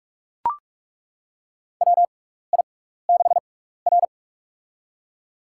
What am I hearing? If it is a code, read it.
Morse code, heard as a keyed tone, about 43 wpm, spaced as for 15 wpm, 700 Hz. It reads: WI6R
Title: USA Call Signs - 4 Characters - 6x Character Spacing - 45wpm